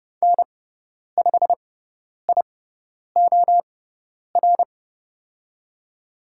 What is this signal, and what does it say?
Morse code, heard as a keyed tone, about 30 wpm, spaced as for 10 wpm, 690 Hz. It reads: N5IOR
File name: Lesson 17g - Letter u - Call Signs 30wpm